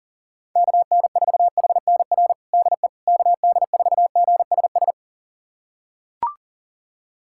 Morse code, heard as a keyed tone, about 40 wpm, 700 Hz. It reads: KN4HNR DE KD4GSS
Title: USA Callsign DE Callsign - 6 Characters - 40wpm